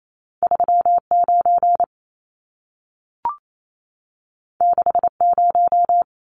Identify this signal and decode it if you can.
Morse code, heard as a keyed tone, about 28 wpm, 700 Hz. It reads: 39 60
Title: Numbers - 2-Digit 28wpm